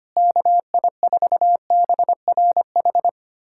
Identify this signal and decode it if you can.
Morse code, heard as a keyed tone, about 25 wpm, 700 Hz. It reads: KI4BRH
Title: USA Callsign DE Callsign 25wpm